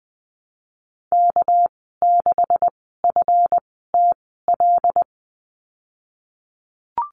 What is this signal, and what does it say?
Morse code, heard as a keyed tone, about 20 wpm, spaced as for 15 wpm, 700 Hz. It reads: K6FTL
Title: 200 Lesson 32g - Letter k - Call Signs 20wpm